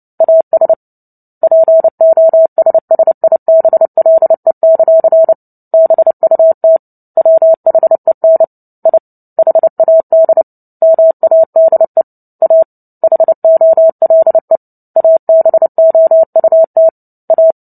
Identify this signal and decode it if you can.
Morse code, heard as a keyed tone, about 30 wpm, 650 Hz. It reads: AS POSSIBLE; BUT WHEN I HAD MADE A HOLE ABOUT A
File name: Book Two / 03-Chapter 3 - 29 WPM